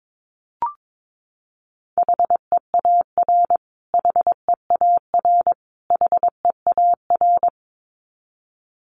Morse code, heard as a keyed tone, about 22 wpm, 700 Hz. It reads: HEAR HEAR HEAR
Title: Top 300 Words Repeated 22wpm